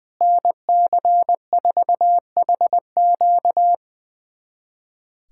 Morse code, heard as a keyed tone, about 20 wpm, 700 Hz. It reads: NC4HQ
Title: USA Callsign DE Callsign - 5 Characters - 20wpm